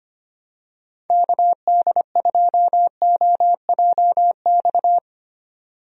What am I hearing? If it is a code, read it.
Morse code, heard as a keyed tone, about 25 wpm, 700 Hz. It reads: KD2OJX
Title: USA Call Signs - 6 Characters - 25wpm